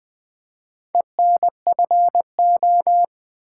Morse code, heard as a keyed tone, about 20 wpm, 700 Hz. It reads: ENFO
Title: Build A Word - 4 Letters At A Time - 11-Letter Words - 20wpm